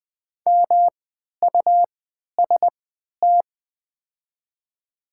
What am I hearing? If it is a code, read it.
Morse code, heard as a keyed tone, about 20 wpm, spaced as for 11 wpm, 700 Hz. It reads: MUST
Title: 3x Character Spacing - Top 200 Words - 20wpm